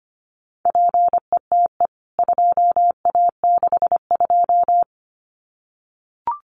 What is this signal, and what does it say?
Morse code, heard as a keyed tone, about 25 wpm, 700 Hz. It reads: PETE 2A62
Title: CWT Exchange 25wpm